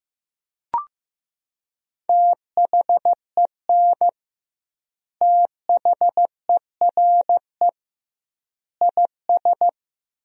Morse code, heard as a keyed tone, about 15 wpm, 700 Hz. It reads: THEN THERE IS
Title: 2x Word-Spacing - Sentences from Top 100 Words - 15wpm